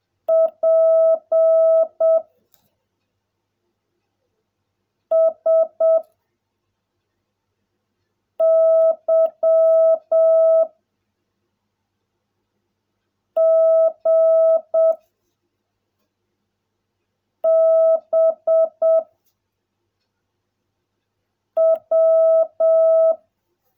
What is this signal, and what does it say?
Morse code, heard as a keyed tone, about 7 wpm, 660 Hz. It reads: P S Y G B W
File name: jh8cbh / 2021cw / cw16a